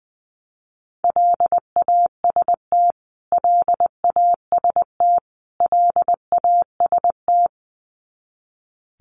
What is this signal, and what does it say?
Morse code, heard as a keyed tone, about 20 wpm, 700 Hz. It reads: LAST LAST LAST
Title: Top 300 Words Repeated 20wpm